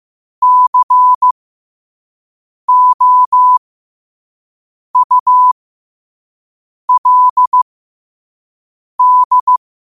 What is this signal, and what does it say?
Morse code, heard as a keyed tone, about 15 wpm, spaced as for 5 wpm, 1000 Hz. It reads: COULD
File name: qotdfiles / 5wpm